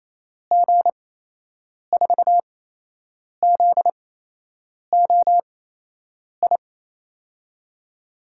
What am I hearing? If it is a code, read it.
Morse code, heard as a keyed tone, about 28 wpm, spaced as for 8 wpm, 700 Hz. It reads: G4ZOI
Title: USA Call Signs - 6 Characters - 8x Character Spacing - 28wpm